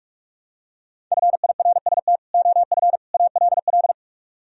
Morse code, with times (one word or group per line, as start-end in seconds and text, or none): 1.1-2.2 FIRST
2.3-3.0 OF
3.1-3.9 ALL